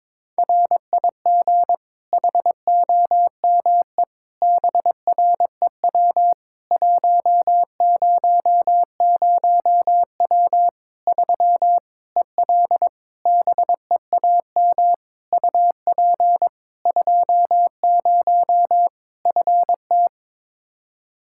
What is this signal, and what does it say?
Morse code, heard as a keyed tone, about 23 wpm, 700 Hz. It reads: RIG HOME BREW 100W 3 EL BEAM UP 20 FT
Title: Practice QSO - Exchange 2 - Info - 22wpm